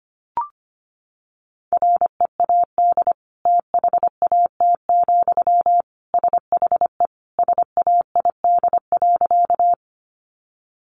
Morse code, heard as a keyed tone, about 25 wpm, 700 Hz. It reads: READ THAT, SHE SAID.
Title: Sentences from Top 200 Words 25wpm